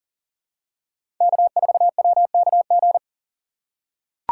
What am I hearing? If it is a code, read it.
Morse code, heard as a keyed tone, about 40 wpm, 700 Hz. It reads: K4WKG
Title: USA Call Signs - 5 Characters - 3x Character Spacing - 40wpm